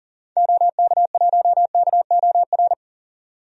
Morse code, heard as a keyed tone, about 39 wpm, 700 Hz. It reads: OK1KOR
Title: DXCC-100 Call Signs - 6 Characters - 40wpm